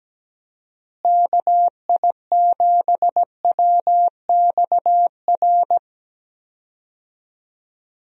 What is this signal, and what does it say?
Morse code, heard as a keyed tone, about 17 wpm, 700 Hz. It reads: KI7WXR
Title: USA Call Signs - 6 Characters - Repeated 3x - 17wpm